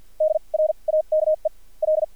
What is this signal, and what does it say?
Morse code, heard as a keyed tone, about 55 wpm, spaced as for 38 wpm, 620 Hz. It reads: DRAKE L
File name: archived / CRSnet